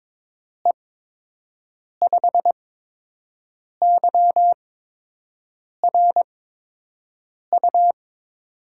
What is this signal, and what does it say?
Morse code, heard as a keyed tone, about 22 wpm, spaced as for 6 wpm, 700 Hz. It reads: E5YRU
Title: USA Call Signs - 6 Characters - 8x Character Spacing - 22wpm